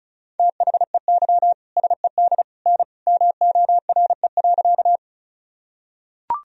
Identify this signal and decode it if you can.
Morse code, heard as a keyed tone, about 34 wpm, 700 Hz. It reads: THEY SED N MORE.
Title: Sentences from Top 100 Words and Abbreviations 35wpm